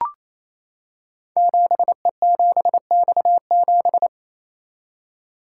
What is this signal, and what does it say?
Morse code, heard as a keyed tone, about 27 wpm, 700 Hz. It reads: 7E7X7
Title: Code Groups - 5 Letter:Numbers - 28wpm